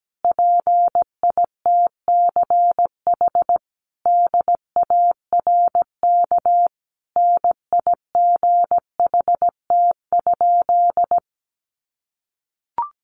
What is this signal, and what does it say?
Morse code, heard as a keyed tone, about 17 wpm, 700 Hz. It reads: PITCH DARK NIGHT?
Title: Sentences from Top 1300 Words 17wpm